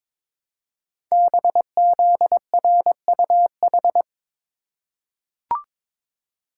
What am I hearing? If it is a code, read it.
Morse code, heard as a keyed tone, about 23 wpm, 700 Hz. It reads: BZRUH
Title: Code Groups - 5 Letters - 22wpm